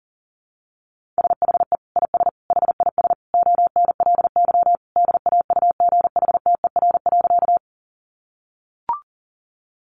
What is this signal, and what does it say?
Morse code, heard as a keyed tone, about 40 wpm, 700 Hz. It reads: SHE IS HIS ONLY DAUGHTER.
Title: Sentences from Top 1500 Words 40wpm